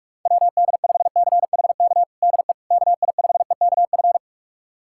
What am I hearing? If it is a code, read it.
Morse code, heard as a keyed tone, about 46 wpm, 700 Hz. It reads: WD5CHK DE KI5EKF
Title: USA Callsign DE Callsign 45wpm